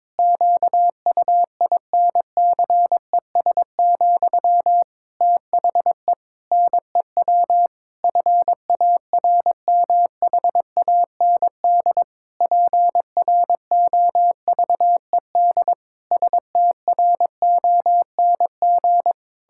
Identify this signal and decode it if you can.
Morse code, heard as a keyed tone, about 22 wpm, 700 Hz. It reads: QUINCES, THE NEW FARMHAND PROVED STRONG
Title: Pangram Challenge v2 - 22wpm